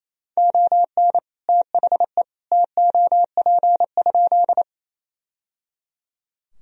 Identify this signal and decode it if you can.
Morse code, heard as a keyed tone, about 28 wpm, 700 Hz. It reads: ON THE TOP?
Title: Sentences from Top 1000 Words - Review Part 2 - 28wpm